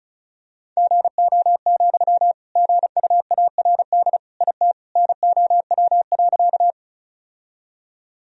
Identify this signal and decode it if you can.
Morse code, heard as a keyed tone, about 36 wpm, 700 Hz. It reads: GO, GUARD IT NOW.
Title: Sentences from Top 1000 Words - Review Part 4 - 35wpm